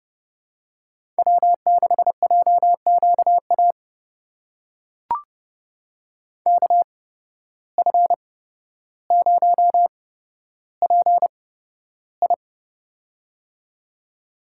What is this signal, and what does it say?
Morse code, heard as a keyed tone, about 30 wpm, spaced as for 13 wpm, 700 Hz. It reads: W6JQA KF0PI
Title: USA Call Signs - 8x Character Spacing - 30wpm